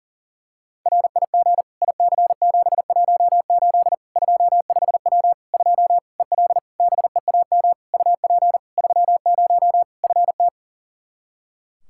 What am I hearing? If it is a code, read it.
Morse code, heard as a keyed tone, about 40 wpm, 700 Hz. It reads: RIG IC718 25W 2 EL BEAM UP 30 FT
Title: Practice QSO - Exchange 2 - Info - 40wpm